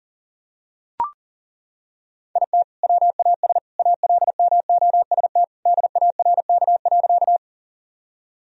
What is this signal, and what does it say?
Morse code, heard as a keyed tone, about 40 wpm, 700 Hz. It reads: IT WAS ALMOST DARK.
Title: Sentences from Top 1000 Words - Review Part 4 - 40wpm